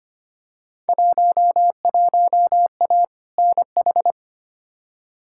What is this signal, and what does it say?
Morse code, heard as a keyed tone, about 25 wpm, 700 Hz. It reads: 11A NH
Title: Field Day Exchanges Encore - 25wpm